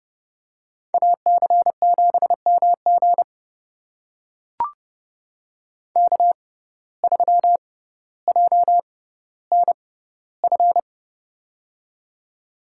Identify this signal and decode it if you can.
Morse code, heard as a keyed tone, about 31 wpm, spaced as for 16 wpm, 700 Hz. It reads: AC7MG K3JNF
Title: USA Call Signs - 5 Characters - 6x Character Spacing - 30wpm